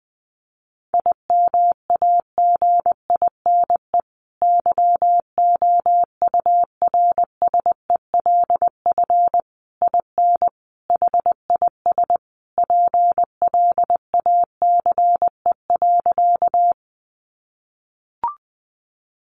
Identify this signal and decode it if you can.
Morse code, heard as a keyed tone, about 20 wpm, 700 Hz. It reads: IMAGINE YOURSELF IN HIS PLACE.
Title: Sentences from Top 1400 Words 20wpm